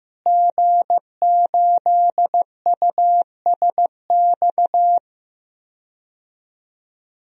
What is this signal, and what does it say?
Morse code, heard as a keyed tone, about 15 wpm, 700 Hz. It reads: G8USX
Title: DXCC-100 Call Signs - 5 Characters - Repeated 3x - 15wpm